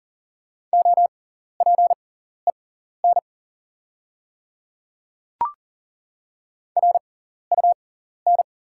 Morse code, heard as a keyed tone, about 40 wpm, spaced as for 14 wpm, 690 Hz. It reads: OPEN RUN
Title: 6x Character Spacing - Top 300 Words - 40wpm